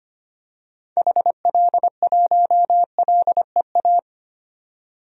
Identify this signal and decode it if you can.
Morse code, heard as a keyed tone, about 25 wpm, 700 Hz. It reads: HL1LEA
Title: DXCC-100 Call Signs - 6 Characters - 25wpm